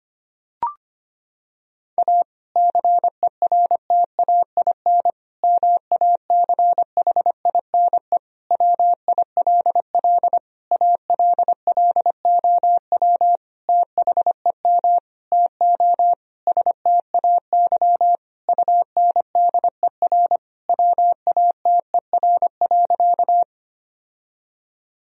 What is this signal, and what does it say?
Morse code, heard as a keyed tone, about 25 wpm, 700 Hz. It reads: A CERTAIN MACHINE WILL ALLOW THEM TO STAY UNDER WATER.
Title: Sentences from Top 1000 Words - Review Part 2 - 25wpm